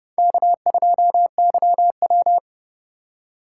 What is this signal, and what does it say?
Morse code, heard as a keyed tone, about 30 wpm, 700 Hz. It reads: K2YW
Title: USA Call Signs - 4 Characters - 3x Character Spacing - 30wpm